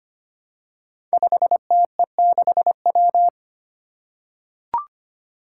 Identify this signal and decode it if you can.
Morse code, heard as a keyed tone, about 25 wpm, 700 Hz. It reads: 5TE6W
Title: Five Challenging Letter-Number Combinations 25wpm